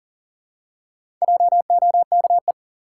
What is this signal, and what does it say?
Morse code, heard as a keyed tone, about 40 wpm, 700 Hz. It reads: JOKE
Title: Four Letter Words - 40wpm